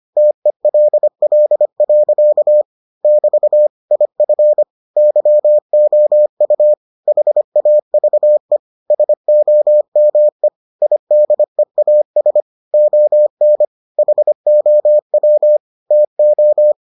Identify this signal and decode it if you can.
Morse code, heard as a keyed tone, about 25 wpm, 600 Hz. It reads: TELL. <BT> IF YOU HAVE SOME IDEAS ON HOW TO